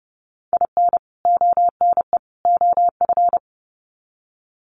Morse code, heard as a keyed tone, about 30 wpm, 700 Hz. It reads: IN ONE OF
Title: Sets of 3 Words 30wpm